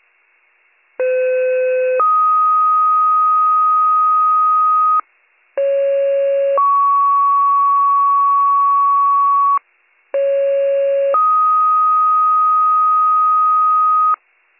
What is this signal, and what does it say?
Two-tone page, heard as a one-second tone, then a three-second tone, about 510 then 1250 Hz, about 550 then 1120 Hz, about 550 then 1250 Hz. It is heard from 1.0 to 14.1 s.